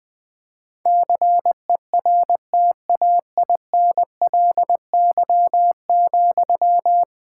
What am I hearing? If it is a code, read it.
Morse code, heard as a keyed tone, about 20 wpm, 700 Hz. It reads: CERTAINLY,
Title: Sentences from Top 1000 Words - Review Part 6 - 20wpm